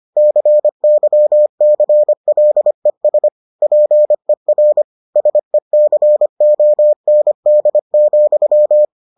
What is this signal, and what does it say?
Morse code, heard as a keyed tone, about 25 wpm, 600 Hz. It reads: CYCLES PER SECOND,